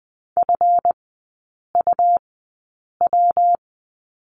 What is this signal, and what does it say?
Morse code, heard as a keyed tone, about 20 wpm, spaced as for 8 wpm, 690 Hz. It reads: FUW
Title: 145 Lesson 24e - Letter y - Three Characters 20wpm